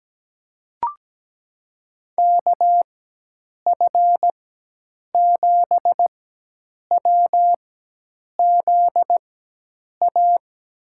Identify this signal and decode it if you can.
Morse code, heard as a keyed tone, about 17 wpm, spaced as for 8 wpm, 700 Hz. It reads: KF7WZA
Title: USA Call Signs - 6 Characters - 4x Character Spacing - 17wpm